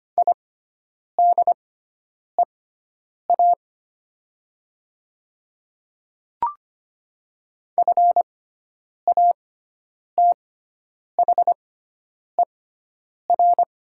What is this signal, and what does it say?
Morse code, heard as a keyed tone, about 25 wpm, spaced as for 9 wpm, 690 Hz. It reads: IDEA FATHER
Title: 6x Character Spacing - Top 300 Words - 25wpm